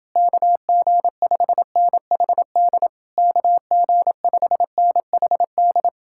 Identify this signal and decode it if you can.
Morse code, heard as a keyed tone, about 29 wpm, 700 Hz. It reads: KG5NHD KG5NHD